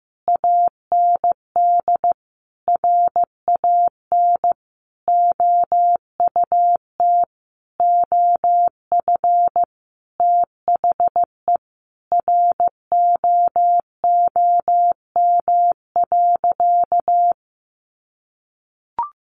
Morse code, heard as a keyed tone, about 15 wpm, 700 Hz. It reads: AND RAN OUT OF THE ROOM.
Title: Sentences from Top 500 Words 15wpm